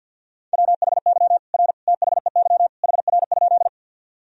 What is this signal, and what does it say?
Morse code, heard as a keyed tone, about 47 wpm, 700 Hz. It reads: WHY R THEY HR?
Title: Sentences from Top 200 Words and Abbreviations 50wpm